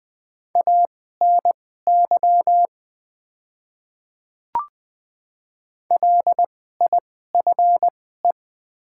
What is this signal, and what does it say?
Morse code, heard as a keyed tone, about 20 wpm, spaced as for 15 wpm, 700 Hz. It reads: ANY LIFE
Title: 2x Character Spacing - Top 500 Words - 20wpm